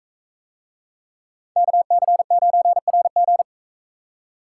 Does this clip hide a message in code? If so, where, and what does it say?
Yes: Morse code from 1.6 to 3.4 s, heard as a keyed tone, about 44 wpm, 700 Hz. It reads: KC9RG